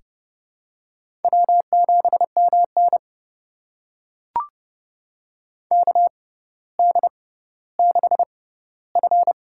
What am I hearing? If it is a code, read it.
Morse code, heard as a keyed tone, about 30 wpm, spaced as for 15 wpm, 700 Hz. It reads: W7MN KD6F
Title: USA Call Signs - 4 Characters - 6x Character Spacing - 30wpm